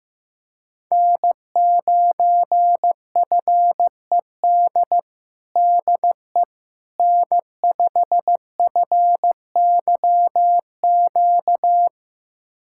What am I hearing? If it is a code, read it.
Morse code, heard as a keyed tone, about 16 wpm, 700 Hz. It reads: N9FED DE N5FYQ
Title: USA Callsign DE Callsign - 5 Characters - 15wpm